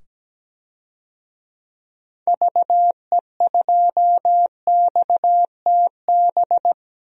Morse code, heard as a keyed tone, about 17 wpm, 700 Hz. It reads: VE2XTB
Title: DXCC-100 Call Signs - 6 Characters - 17wpm